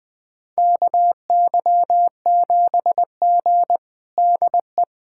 Morse code, heard as a keyed tone, about 21 wpm, 700 Hz. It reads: KY7G DE
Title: USA Callsign DE Callsign - 4 Characters - 20wpm